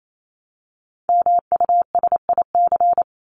Morse code, heard as a keyed tone, about 28 wpm, 700 Hz. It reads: MUSIC
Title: Top 300 Words 28wpm